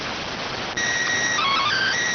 SSTV (slow-scan television), PD 290.